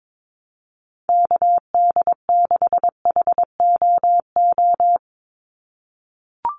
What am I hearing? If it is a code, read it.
Morse code, heard as a keyed tone, about 22 wpm, 700 Hz. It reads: KD6HOO
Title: USA Call Signs 22wpm